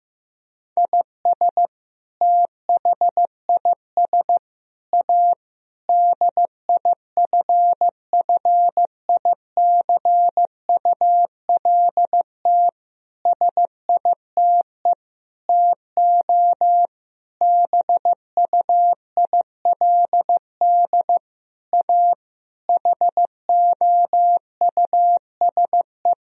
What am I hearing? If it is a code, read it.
Morse code, heard as a keyed tone, about 16 wpm, 700 Hz. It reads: IS THIS A DIFFICULT SITE TO BUILD A HOUSE
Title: Sentences from Top 1000 Words - Review Part 4 - 15wpm